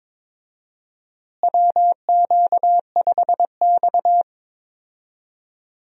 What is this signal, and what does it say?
Morse code, heard as a keyed tone, about 22 wpm, 700 Hz. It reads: WQ5X
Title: USA Call Signs - 4 Characters - Repeated 2x - 22wpm